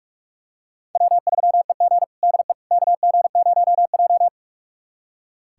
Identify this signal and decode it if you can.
Morse code, heard as a keyed tone, about 46 wpm, 700 Hz. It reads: W3EG DE KG0J
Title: USA Callsign DE Callsign - 4 Characters - 45wpm